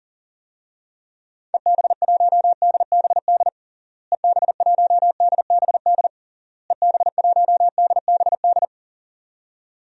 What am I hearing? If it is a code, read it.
Morse code, heard as a keyed tone, about 40 wpm, 700 Hz. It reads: EB1DBD EB1DBD EB1DBD
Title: DXCC-100 Call Signs - 6 Characters - Repeated 3x - 40wpm